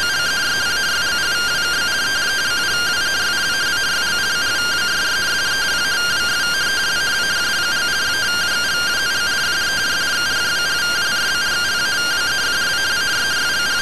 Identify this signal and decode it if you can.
Two-tone teleprinter signal, about 170 Hz shift.